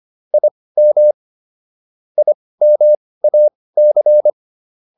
Morse code, heard as a keyed tone, about 25 wpm, spaced as for 18 wpm, 600 Hz. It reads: IM IMAC